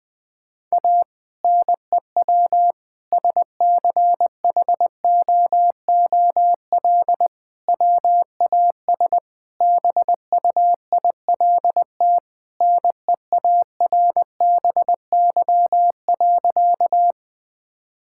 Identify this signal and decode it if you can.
Morse code, heard as a keyed tone, about 21 wpm, 700 Hz. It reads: A NEW SCHOOL WAS BUILT NEARBY.
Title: Sentences with 2-Word Phrases - 20wpm